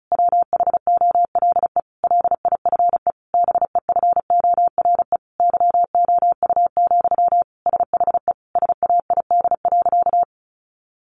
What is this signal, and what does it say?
Morse code, heard as a keyed tone, about 34 wpm, 700 Hz. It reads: WHOLE LIFE BEFORE YOU, SHE SAID.
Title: Sentences from Top 400 Words 35wpm